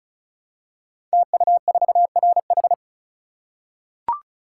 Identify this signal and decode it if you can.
Morse code, heard as a keyed tone, about 36 wpm, 700 Hz. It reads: TU4RH
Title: Code Groups - 5 Letter:Numbers - 35wpm